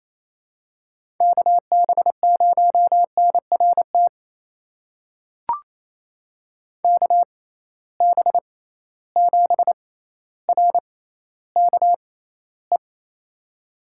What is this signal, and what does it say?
Morse code, heard as a keyed tone, about 28 wpm, spaced as for 14 wpm, 700 Hz. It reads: KB0NRT KB7RKE
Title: USA Call Signs - 6 Characters - 6x Character Spacing - 28wpm